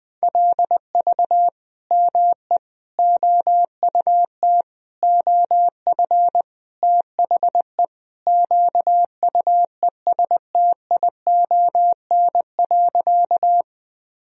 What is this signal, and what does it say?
Morse code, heard as a keyed tone, about 20 wpm, 700 Hz. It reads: LV ME OUT OF THE QUESTION.
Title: Sentences from Top 400 Words and Abbreviations 20wpm